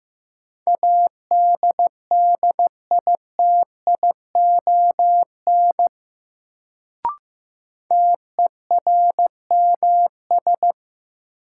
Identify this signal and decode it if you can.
Morse code, heard as a keyed tone, about 15 wpm, 700 Hz. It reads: ADDITION TERMS
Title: Sets of 3 Words - New Word Drill - 15wpm